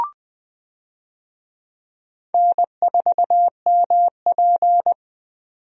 Morse code, heard as a keyed tone, about 20 wpm, 700 Hz. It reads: N4MP
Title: USA Call Signs - 4 Characters - 20wpm